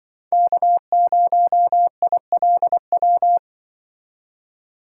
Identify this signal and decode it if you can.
Morse code, heard as a keyed tone, about 24 wpm, 700 Hz. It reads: K0ILW
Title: Speed Racing - USA Call Signs 20wpm